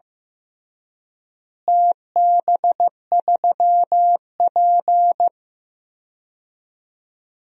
Morse code, heard as a keyed tone, about 15 wpm, 700 Hz. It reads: TB3P
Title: DXCC-100 Call Signs - 4 Characters - Repeated 3x - 15wpm